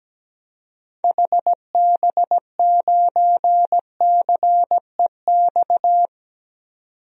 Morse code, heard as a keyed tone, about 17 wpm, 700 Hz. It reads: HB9CEX
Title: DXCC-100 Call Signs - 6 Characters - 17wpm